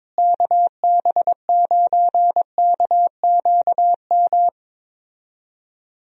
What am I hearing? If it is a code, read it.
Morse code, heard as a keyed tone, about 22 wpm, 700 Hz. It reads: KB9KQM
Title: USA Call Signs - 6 Characters - Repeated 2x - 22wpm